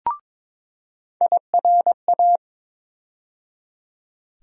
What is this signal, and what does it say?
Morse code, heard as a keyed tone, about 21 wpm, 700 Hz. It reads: IRA
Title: USA Call Sign Postfixes - 3 Letters 22wpm